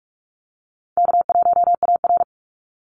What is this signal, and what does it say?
Morse code, heard as a keyed tone, about 44 wpm, 700 Hz. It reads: K1AR
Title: Easy Call Signs Repeated 3x - 45wpm